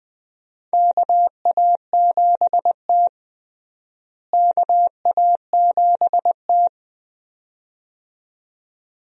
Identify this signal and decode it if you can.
Morse code, heard as a keyed tone, about 21 wpm, 700 Hz. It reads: KA7T KA7T
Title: USA Call Signs - 4 Characters - Repeated 2x - 20wpm